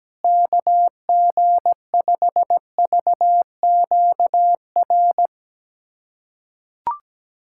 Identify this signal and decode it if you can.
Morse code, heard as a keyed tone, about 17 wpm, 700 Hz. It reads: KG5VQR
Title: USA Callsign DE Callsign - 6 Characters - 17wpm